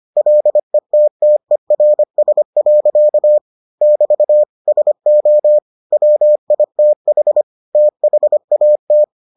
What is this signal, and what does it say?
Morse code, heard as a keyed tone, about 25 wpm, 600 Hz. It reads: LETTERS. <BT> SO WITH THAT